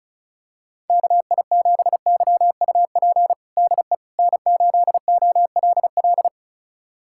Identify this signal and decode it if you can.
Morse code, heard as a keyed tone, about 36 wpm, 700 Hz. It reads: KI7YUP DE N8OLL
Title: USA Callsign DE Callsign 35wpm